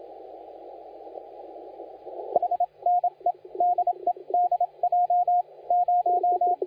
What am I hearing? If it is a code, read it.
Morse code, heard as a keyed tone, about 27 wpm, 710 Hz, with radio noise. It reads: H NE DE DJ 0
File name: la4c / sac-cw-2018 / CONTEST AUDIO 1700Z